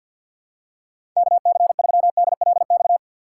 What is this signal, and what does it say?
Morse code, heard as a keyed tone, about 53 wpm, 700 Hz. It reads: KC3DLX
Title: USA Call Signs - 6 Characters - 6x Character Spacing - 50wpm